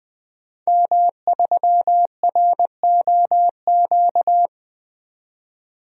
Morse code, heard as a keyed tone, about 20 wpm, 700 Hz. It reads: M3ROQ
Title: DXCC-100 Call Signs - 5 Characters - 20wpm